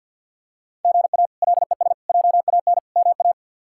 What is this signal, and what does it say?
Morse code, heard as a keyed tone, about 52 wpm, 700 Hz. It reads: GA LES JAN MA
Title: K1USN SST QSO Practice - 50wpm